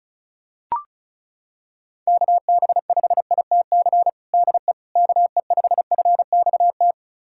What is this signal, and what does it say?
Morse code, heard as a keyed tone, about 36 wpm, 700 Hz. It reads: KB5ITC DE KE5FXT
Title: USA Callsign DE Callsign - 6 Characters - 35wpm